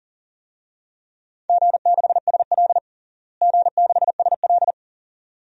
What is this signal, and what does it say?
Morse code, heard as a keyed tone, about 37 wpm, 700 Hz. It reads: G6SL G6SL
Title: DXCC-100 Call Signs - 4 Characters - Repeated 2x - 40wpm